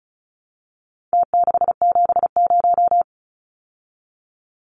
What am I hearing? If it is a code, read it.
Morse code, heard as a keyed tone, about 34 wpm, 700 Hz. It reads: T670
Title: Numbers - Contest Style - 4-Digit 35wpm